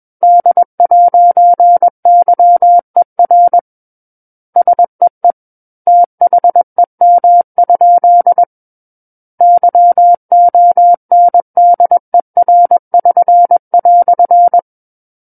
Morse code, heard as a keyed tone, about 21 wpm, 700 Hz. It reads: D'YER SEE THEM? YONDER<SN>"
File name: Book One / 12-Chapter 12 - 21 WPM